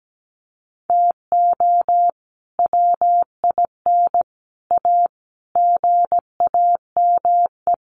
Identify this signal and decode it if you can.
Morse code, heard as a keyed tone, about 17 wpm, 700 Hz. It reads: TO WIN A GAME
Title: Sentences from Top 1500 Words 17wpm